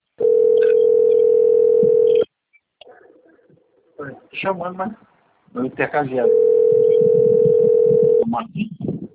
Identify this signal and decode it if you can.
Ringback tone (North America), 440 + 480 Hz, with speech.
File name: rg-1600-unknown- - - .20937